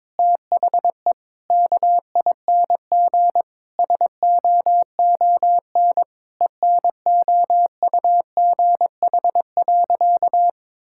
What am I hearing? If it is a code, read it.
Morse code, heard as a keyed tone, about 22 wpm, 700 Hz. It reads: THE KING SOON ENOUGH.
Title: Sentences from Top 1000 Words - Review Part 6 - 22wpm